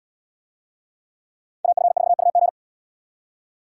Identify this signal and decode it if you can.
Morse code, heard as a keyed tone, about 71 wpm, 700 Hz. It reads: AF4UG